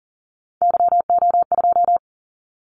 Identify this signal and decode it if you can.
Morse code, heard as a keyed tone, about 40 wpm, 700 Hz. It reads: YO2
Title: Callsign Prefixes for Top 25 DXCC - 40wpm